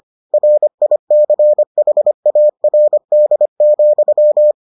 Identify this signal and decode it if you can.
Morse code, heard as a keyed tone, about 25 wpm, 600 Hz. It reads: RICHARD,